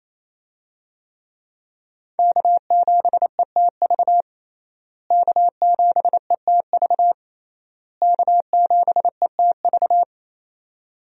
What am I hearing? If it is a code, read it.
Morse code, heard as a keyed tone, about 28 wpm, 700 Hz. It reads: K7ETV K7ETV K7ETV
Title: USA Call Signs - 5 Characters - Repeated 3x - 28wpm